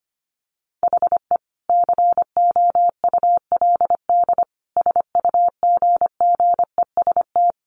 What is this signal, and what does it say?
Morse code, heard as a keyed tone, about 25 wpm, 700 Hz. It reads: HE COULD SUGGEST
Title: Sentences from Top 1300 Words 25wpm